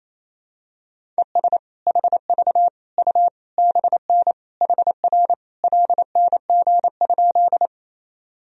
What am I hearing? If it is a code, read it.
Morse code, heard as a keyed tone, about 28 wpm, 700 Hz. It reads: ES HV U BN HR LNG?
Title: Sentences from Top 200 Words and Abbreviations 28wpm